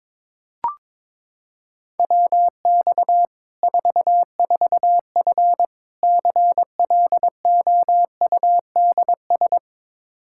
Morse code, heard as a keyed tone, about 22 wpm, 700 Hz. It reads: WX 44F CLOUDS
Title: K1USN SST QSO Practice 22wpm